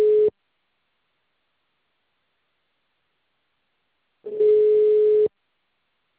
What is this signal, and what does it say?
Ringback tone (Europe, 425 Hz).